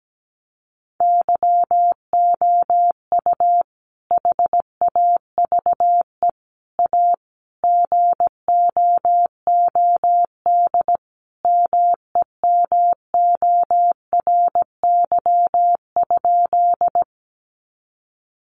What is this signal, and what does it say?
Morse code, heard as a keyed tone, about 17 wpm, 700 Hz. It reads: YOU HAVE A GOOD MEMORY?
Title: Sentences from Top 1300 Words 17wpm